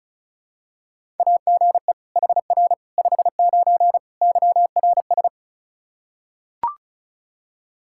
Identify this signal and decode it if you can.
Morse code, heard as a keyed tone, about 34 wpm, 700 Hz. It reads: AGE HR 59 YRS
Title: Practice QSO - Exchange 3 - More Info - 35wpm